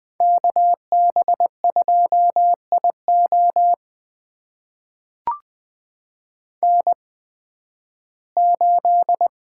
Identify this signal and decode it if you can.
Morse code, heard as a keyed tone, about 20 wpm, spaced as for 13 wpm, 700 Hz. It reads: KB2IO N8
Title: USA Call Signs - 5 Characters - 8x Character Spacing - 20wpm